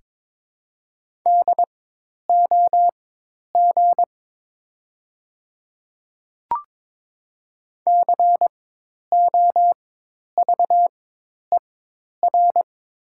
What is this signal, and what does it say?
Morse code, heard as a keyed tone, about 22 wpm, spaced as for 10 wpm, 700 Hz. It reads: DOG COVER
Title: 4x Character Spacing - Top 500 Words - 22wpm